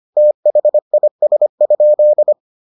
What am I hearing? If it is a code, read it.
Morse code, heard as a keyed tone, about 25 wpm, 600 Hz. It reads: THIS?